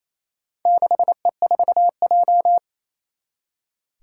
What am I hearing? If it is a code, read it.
Morse code, heard as a keyed tone, about 28 wpm, 700 Hz. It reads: 6E4J
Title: Four Challenging Letter-Number Combinations 28wpm